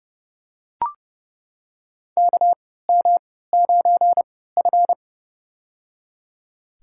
Morse code, heard as a keyed tone, about 30 wpm, spaced as for 17 wpm, 700 Hz. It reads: KM9F
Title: USA Call Signs - 4 Characters - 3x Character Spacing - 30wpm